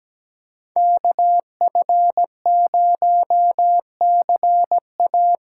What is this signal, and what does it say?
Morse code, heard as a keyed tone, about 17 wpm, 700 Hz. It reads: KF0CA
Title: USA Callsign DE Callsign - 5 Characters - 17wpm